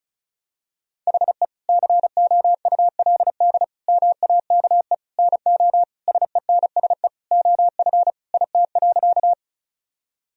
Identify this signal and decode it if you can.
Morse code, heard as a keyed tone, about 34 wpm, 700 Hz. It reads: HE COULD MAKE NO SENSE OF IT.
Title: Sentences from Top 1000 Words - Review Part 4 - 35wpm